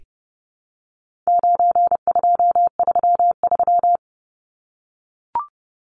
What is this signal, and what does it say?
Morse code, heard as a keyed tone, about 29 wpm, 700 Hz. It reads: 9233
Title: Numbers - 4-Digit 30wpm